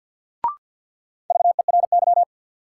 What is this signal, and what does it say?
Morse code, heard as a keyed tone, about 52 wpm, 700 Hz. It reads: VERY
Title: Sets of 2 Words - New Word Drill - 50wpm